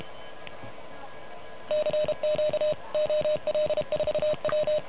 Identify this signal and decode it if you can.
Morse code, heard as a keyed tone, about 32 wpm, 600 Hz. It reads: CQ OL4W
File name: truSDX / MWC29.8